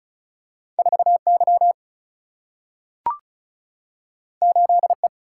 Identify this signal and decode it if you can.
Morse code, heard as a keyed tone, about 36 wpm, 700 Hz. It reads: 4Y 8E
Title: Single-Letter Single-Number 35wpm